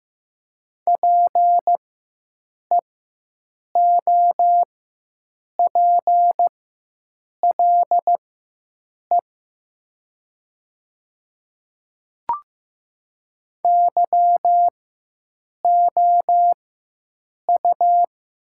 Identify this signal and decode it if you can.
Morse code, heard as a keyed tone, about 15 wpm, spaced as for 7 wpm, 700 Hz. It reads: PEOPLE YOU
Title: 4x Character Spacing - Top 100 Words - 15wpm